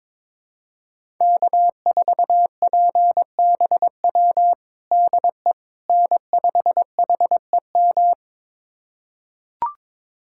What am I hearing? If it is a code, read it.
Morse code, heard as a keyed tone, about 21 wpm, 700 Hz. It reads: K4PBW DE N5HEM
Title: USA Callsign DE Callsign - 5 Characters - 22wpm